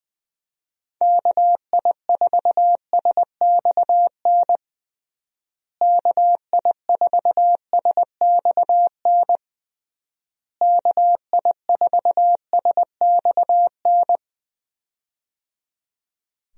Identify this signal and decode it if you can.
Morse code, heard as a keyed tone, about 20 wpm, 700 Hz. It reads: KI4SXN KI4SXN KI4SXN
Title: USA Call Signs - 6 Characters - Repeated 3x - 20wpm